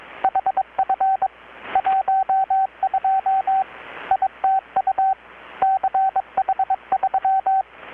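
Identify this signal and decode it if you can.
Morse code, heard as a keyed tone, about 21 wpm, 770 Hz, with radio noise. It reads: HF 12 ITU CH3
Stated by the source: Morse from KPH's Night of Nights XX maritime radiotelegraphy event, received in Chiapas, Mexico about 3,800 km away on a Kenwood R-600 with a grounded 10 m long-wire antenna.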